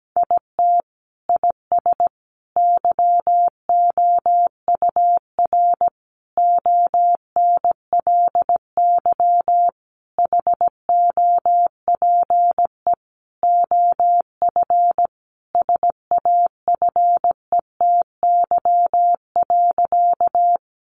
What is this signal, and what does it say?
Morse code, heard as a keyed tone, about 17 wpm, 700 Hz. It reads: IT IS YOUR ONLY HOPE OF SAFETY.
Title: Sentences from Top 1000 Words 17wpm